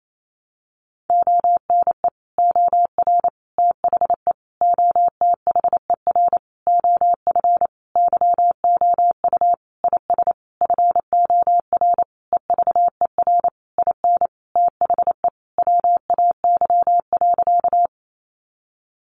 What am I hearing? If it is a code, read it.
Morse code, heard as a keyed tone, about 28 wpm, 700 Hz. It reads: ONE OR THE OTHER OF YOU IS FOR EVER IN THE WAY.
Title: Sentences from Top 400 Words 28wpm